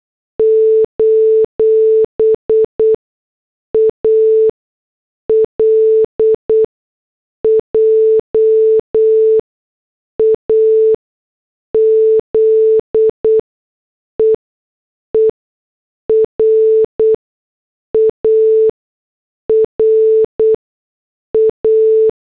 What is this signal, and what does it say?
Morse code, heard as a keyed tone, about 8 wpm, spaced as for 6 wpm, 440 Hz. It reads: : ALJAZEERA RA